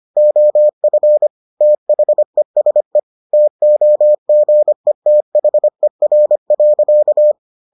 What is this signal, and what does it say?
Morse code, heard as a keyed tone, about 25 wpm, 600 Hz. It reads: OF THESE TOGETHER.